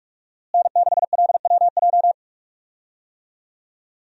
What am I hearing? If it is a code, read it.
Morse code, heard as a keyed tone, about 46 wpm, 700 Hz. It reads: N6LWJ
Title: USA Call Signs - 5 Characters - Repeated 3x - 45wpm